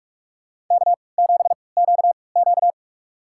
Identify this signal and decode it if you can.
Morse code, heard as a keyed tone, about 44 wpm, spaced as for 25 wpm, 700 Hz. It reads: K7QQ
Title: USA Call Signs - 4 Characters - 3x Character Spacing - 45wpm